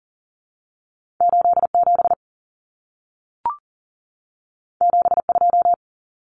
Morse code, heard as a keyed tone, about 40 wpm, 700 Hz. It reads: 87 72
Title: Numbers - 2-Digit 40wpm